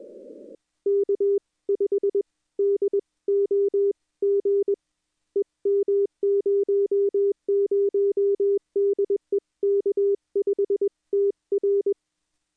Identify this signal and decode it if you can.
Morse code, heard as a keyed tone, about 21 wpm, spaced as for 18 wpm, 400 Hz. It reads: K5DOG EM00 DE K5TR
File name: contests / 2019-june=vhf-k5tr- -0101 0-2